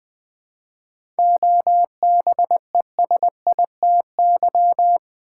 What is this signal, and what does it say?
Morse code, heard as a keyed tone, about 20 wpm, 700 Hz. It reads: OBESITY
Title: 7 Letter Words - First 3 Letters Given - 20wpm